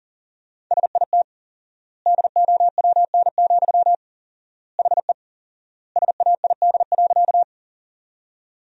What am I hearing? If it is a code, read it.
Morse code, heard as a keyed tone, about 40 wpm, 700 Hz. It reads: SIT DOWN, HE SAID.
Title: 4x Word-Spacing - Sentences from Top 500 Words - 40wpm